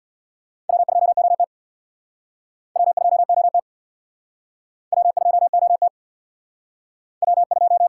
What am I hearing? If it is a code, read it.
Morse code, heard as a keyed tone, about 68 wpm, 690 Hz. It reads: W2OT W2OT W2OT W2